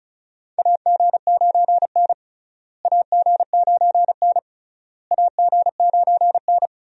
Morse code, heard as a keyed tone, about 36 wpm, 700 Hz. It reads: AG9N AG9N AG9N
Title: USA Call Signs - 4 Characters - Repeated 3x - 35wpm